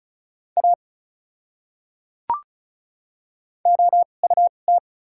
Morse code, heard as a keyed tone, about 36 wpm, spaced as for 25 wpm, 700 Hz. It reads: A OUT
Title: 2x Character Spacing - Top 200 Words - 35wpm